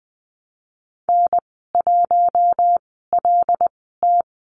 Morse code, heard as a keyed tone, about 19 wpm, spaced as for 14 wpm, 700 Hz. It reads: N1LT
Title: 088 Lesson 15g - Number 5 - Call Signs 20wpm